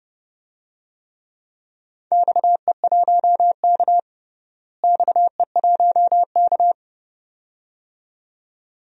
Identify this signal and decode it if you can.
Morse code, heard as a keyed tone, about 30 wpm, 700 Hz. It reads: XE1K XE1K
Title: DXCC-100 Call Signs - 4 Characters - Repeated 2x - 30wpm